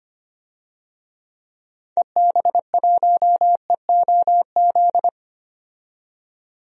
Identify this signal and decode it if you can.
Morse code, heard as a keyed tone, about 25 wpm, 700 Hz. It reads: EB1EOZ
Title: DXCC-100 Call Signs - 6 Characters - 25wpm